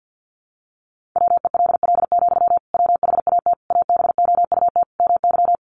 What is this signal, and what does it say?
Morse code, heard as a keyed tone, about 50 wpm, 700 Hz. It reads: WELL, WHAT ABOUT MY
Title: Sentences from Top 500 Words 50wpm